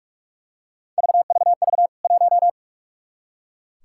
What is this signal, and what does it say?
Morse code, heard as a keyed tone, about 44 wpm, 700 Hz. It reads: VVV 1
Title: K1USN SST QSO Practice 45wpm